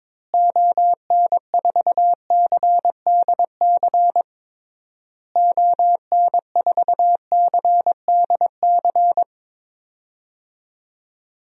Morse code, heard as a keyed tone, about 22 wpm, 700 Hz. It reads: ON4CDC ON4CDC
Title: DXCC-100 Call Signs - 6 Characters - Repeated 3x - 22wpm